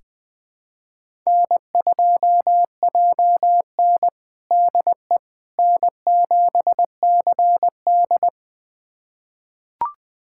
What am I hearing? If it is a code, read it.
Morse code, heard as a keyed tone, about 21 wpm, 700 Hz. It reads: N2JN DE N7CD
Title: USA Callsign DE Callsign - 4 Characters - 20wpm